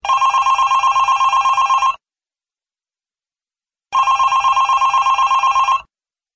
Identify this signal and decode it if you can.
There is no code.